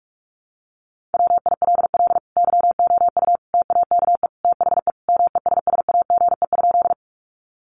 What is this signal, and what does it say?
Morse code, heard as a keyed tone, about 44 wpm, 700 Hz. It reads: WILL YOU TAKE THE MESSAGE?
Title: Sentences from Top 1100 Words 45wpm